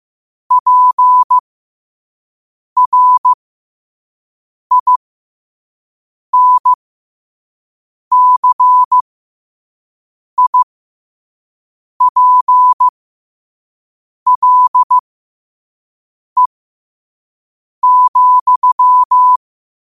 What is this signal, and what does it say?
Morse code, heard as a keyed tone, about 15 wpm, spaced as for 5 wpm, 1000 Hz. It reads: PRINCIPLE,